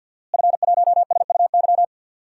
Morse code, heard as a keyed tone, about 51 wpm, 700 Hz. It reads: F1SUY
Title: DXCC-100 Call Signs - 5 Characters - Repeated 2x - 50wpm